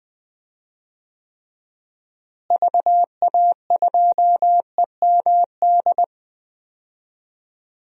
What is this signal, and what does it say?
Morse code, heard as a keyed tone, about 20 wpm, 700 Hz. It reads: VA2EMD
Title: DXCC-100 Call Signs - 6 Characters - Repeated 3x - 20wpm